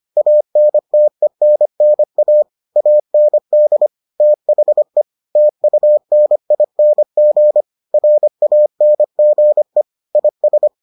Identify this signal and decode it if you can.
Morse code, heard as a keyed tone, about 25 wpm, 600 Hz. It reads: ANTENNA AND THE TUNING RANGE IS